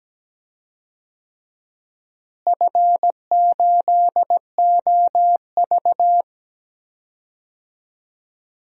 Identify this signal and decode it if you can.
Morse code, heard as a keyed tone, about 17 wpm, 700 Hz. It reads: F8OV
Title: DXCC-100 Call Signs - 4 Characters - Repeated 3x - 17wpm